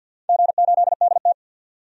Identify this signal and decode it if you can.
Morse code, heard as a keyed tone, about 52 wpm, 700 Hz. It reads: G8DT
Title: DXCC-100 Call Signs - 4 Characters - 50wpm